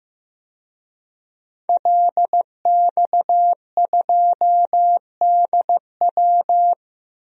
Morse code, heard as a keyed tone, about 16 wpm, 700 Hz. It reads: LX2DW
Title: DXCC-100 Call Signs - 5 Characters - Repeated 2x - 15wpm